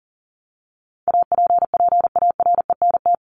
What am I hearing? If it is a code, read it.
Morse code, heard as a keyed tone, about 40 wpm, 700 Hz. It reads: APPARENT
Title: LY Suffix - Top 100 Words with Base Words 40wpm